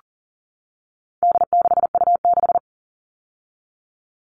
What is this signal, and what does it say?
Morse code, heard as a keyed tone, about 40 wpm, 700 Hz. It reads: D6U6
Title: All Cut Numbers - 4-Digit 40wpm